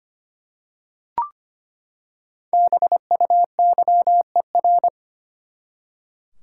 Morse code, heard as a keyed tone, about 25 wpm, 700 Hz. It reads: BUYER
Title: Five Letter Words - 25wpm